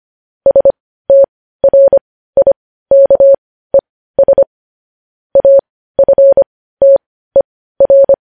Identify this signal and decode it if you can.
Morse code, heard as a keyed tone, about 25 wpm, spaced as for 15 wpm, 550 Hz. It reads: STRIKES AFTER
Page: News Headlines in Morse Code at 15 WPM